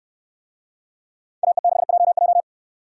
Morse code, heard as a keyed tone, about 68 wpm, 700 Hz. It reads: AE6JJ